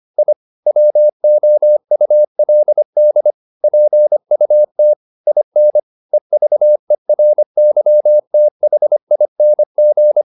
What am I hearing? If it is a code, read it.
Morse code, heard as a keyed tone, about 25 wpm, 600 Hz. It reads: I WOULD PUT IN EVERYTHING
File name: the old man and the sea part 1 25wpm